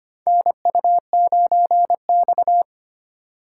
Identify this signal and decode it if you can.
Morse code, heard as a keyed tone, about 25 wpm, 700 Hz. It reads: NU9X
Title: USA Call Signs - 4 Characters - 3x Character Spacing - 25wpm